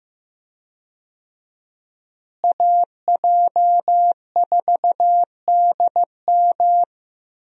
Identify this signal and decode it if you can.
Morse code, heard as a keyed tone, about 15 wpm, 700 Hz. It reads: AJ4DM
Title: USA Call Signs - 5 Characters - Repeated 2x - 15wpm